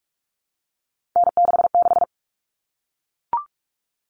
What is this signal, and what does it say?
Morse code, heard as a keyed tone, about 44 wpm, 700 Hz. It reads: N66
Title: All Cut Numbers - 3-Digit 45wpm